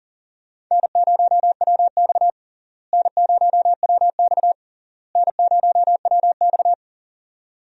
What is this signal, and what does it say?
Morse code, heard as a keyed tone, about 40 wpm, 700 Hz. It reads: N0WX N0WX N0WX
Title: USA Call Signs - 4 Characters - Repeated 3x - 40wpm